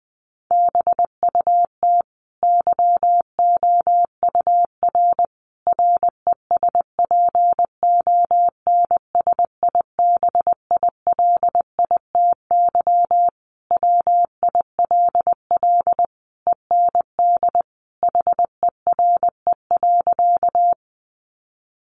Morse code, heard as a keyed tone, about 20 wpm, 700 Hz. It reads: BUT YOUR RESPONSIBILITY WILL END HERE.
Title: Sentences from Top 1200 Words 20wpm